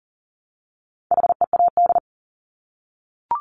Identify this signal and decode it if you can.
Morse code, heard as a keyed tone, about 40 wpm, 700 Hz. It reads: HEAD
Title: Top 500 Words Repeated 40wpm